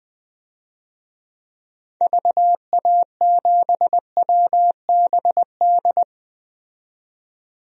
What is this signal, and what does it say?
Morse code, heard as a keyed tone, about 20 wpm, 700 Hz. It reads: VA7WBD
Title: DXCC-100 Call Signs - 6 Characters - 20wpm